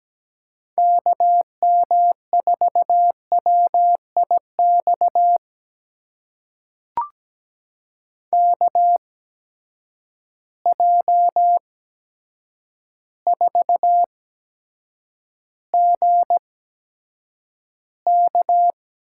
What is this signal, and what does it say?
Morse code, heard as a keyed tone, about 17 wpm, spaced as for 8 wpm, 700 Hz. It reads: KM4WIX KJ4GK
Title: USA Call Signs - 6 Characters - 8x Character Spacing - 17wpm